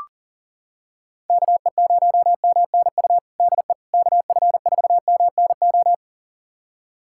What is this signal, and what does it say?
Morse code, heard as a keyed tone, about 40 wpm, 700 Hz. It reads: KE0MNU DE KF4MNO
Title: USA Callsign DE Callsign - 6 Characters - 40wpm